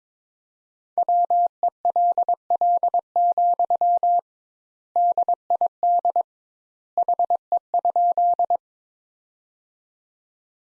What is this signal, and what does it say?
Morse code, heard as a keyed tone, about 22 wpm, 700 Hz. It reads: WELL, DID HE?
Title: Sentences Top 200 Words - 22wpm